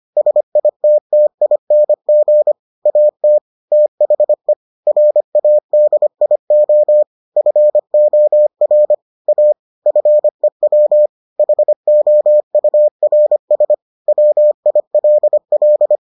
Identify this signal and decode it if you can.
Morse code, heard as a keyed tone, about 25 wpm, 600 Hz. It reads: SITTING AT THE RADIO FOR A FEW HOURS WILL